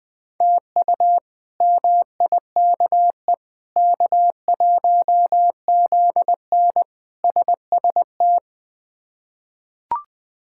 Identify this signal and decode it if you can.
Morse code, heard as a keyed tone, about 21 wpm, 700 Hz. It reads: TU MIKE K1ZN SST
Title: K1USN SST QSO Practice - 20wpm